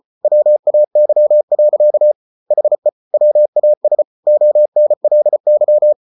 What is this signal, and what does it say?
Morse code, heard as a keyed tone, about 34 wpm, 600 Hz. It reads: WAY. HE WAS ONLY